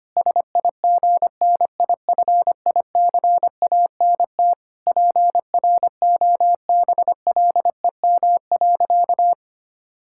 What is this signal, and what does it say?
Morse code, heard as a keyed tone, about 25 wpm, 700 Hz. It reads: SIGNIFICANT PROBLEM.
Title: Sentences from Top 1000 Words - Review Part 6 - 25wpm